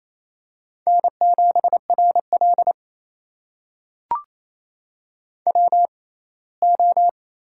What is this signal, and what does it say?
Morse code, heard as a keyed tone, about 27 wpm, spaced as for 19 wpm, 700 Hz. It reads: N7RL WO
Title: USA Call Signs - 4 Characters - 6x Character Spacing - 28wpm